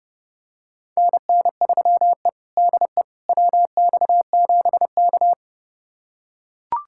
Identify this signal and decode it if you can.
Morse code, heard as a keyed tone, about 32 wpm, 700 Hz. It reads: NN3E DE WX7K
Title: USA Callsign DE Callsign - 4 Characters - 30wpm